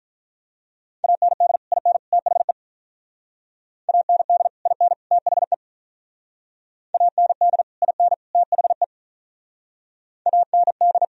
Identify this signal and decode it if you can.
Morse code, heard as a keyed tone, about 46 wpm, 700 Hz. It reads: AND IN THE AND IN THE AND IN THE AND